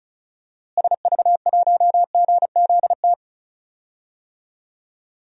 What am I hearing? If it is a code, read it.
Morse code, heard as a keyed tone, about 34 wpm, 700 Hz. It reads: SV1GZT
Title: DXCC-100 Call Signs - 6 Characters - Repeated 3x - 35wpm